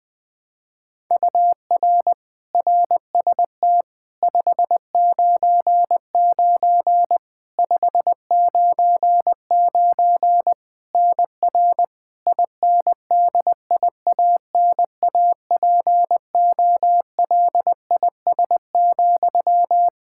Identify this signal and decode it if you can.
Morse code, heard as a keyed tone, about 21 wpm, 700 Hz. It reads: UR RST 599 599 NR INDIANAPOLIS,
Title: Practice QSO - Exchange 1 - Beginning - 20wpm